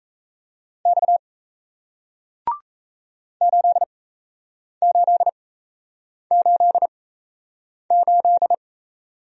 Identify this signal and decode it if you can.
Morse code, heard as a keyed tone, about 37 wpm, spaced as for 8 wpm, 700 Hz. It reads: X 8888